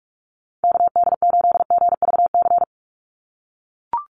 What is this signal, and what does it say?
Morse code, heard as a keyed tone, about 44 wpm, 700 Hz. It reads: KD8GVC
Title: USA Call Signs 45wpm